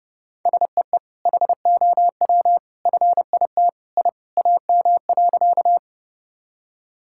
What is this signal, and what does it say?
Morse code, heard as a keyed tone, about 30 wpm, 700 Hz. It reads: SEE HOW FIT I AM.
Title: Sentences from Top 1000 Words - Review Part 5 - 30wpm